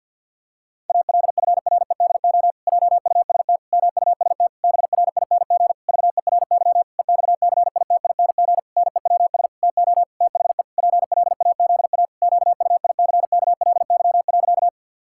Morse code, heard as a keyed tone, about 51 wpm, 700 Hz. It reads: ALFREDO JUST MUST BRING VERY EXCITING NEWS TO THE PLAZA QUICKLY.
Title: Pangram Challenge v2 - 50wpm